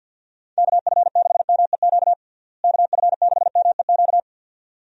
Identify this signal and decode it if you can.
Morse code, heard as a keyed tone, about 47 wpm, 700 Hz. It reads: KF6MEQ KF6MEQ
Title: USA Call Signs - 6 Characters - Repeated 2x - 50wpm